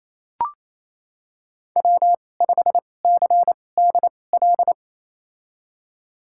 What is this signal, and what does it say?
Morse code, heard as a keyed tone, about 28 wpm, spaced as for 20 wpm, 700 Hz. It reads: W5CDL
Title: USA Call Signs - 5 Characters - 2x Character Spacing - 28wpm